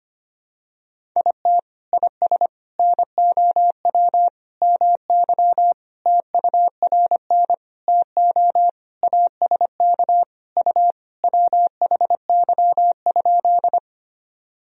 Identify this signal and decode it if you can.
Morse code, heard as a keyed tone, about 25 wpm, 700 Hz. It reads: IT IS NOW MY TURN TO ASK U WHY?
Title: Sentences from Top 200 Words and Abbreviations 25wpm